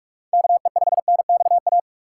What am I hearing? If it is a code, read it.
Morse code, heard as a keyed tone, about 41 wpm, 700 Hz. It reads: KE5NXA
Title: USA Call Signs - 2x Character Spacing - 45wpm